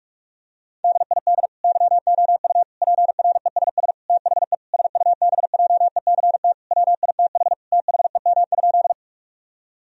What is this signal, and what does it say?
Morse code, heard as a keyed tone, about 46 wpm, 700 Hz. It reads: DID YOU PRESS THE SUBJECT WITH THEM?
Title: Sentences from Top 1000 Words - Review Part 1 - 45wpm